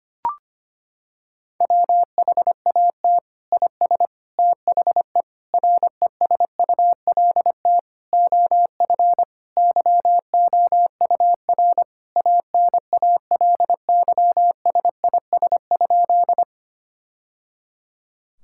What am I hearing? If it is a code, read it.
Morse code, heard as a keyed tone, about 25 wpm, 700 Hz. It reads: WHAT IS THE RESULT OF YOUR ANALYSIS?
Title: Sentences from Top 1000 Words - Review Part 5 - 25wpm